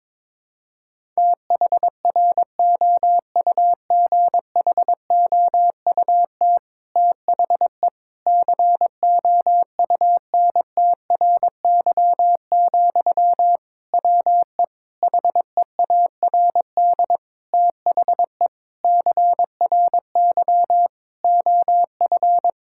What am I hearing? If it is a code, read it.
Morse code, heard as a keyed tone, about 22 wpm, 700 Hz. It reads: THROUGHOUT THE COUNTRY, WE HEARD THE CRY OF
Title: Sentences from Top 1000 Words - Review Part 5 - 22wpm